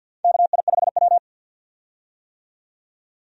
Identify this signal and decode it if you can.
Morse code, heard as a keyed tone, about 52 wpm, 700 Hz. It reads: KI5W
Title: USA Call Signs - 4 Characters - 50wpm